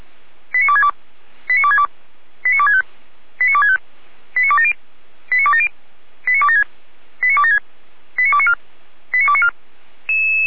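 Five-tone selective call (ZVEI), heard as quick runs of five tones, tones of about 70 ms. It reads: 89262 89262 89265 89265 89270 89270 89276 89276 89294 89294
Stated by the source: Five-tone ZVEI selective call 89280, received by SDR.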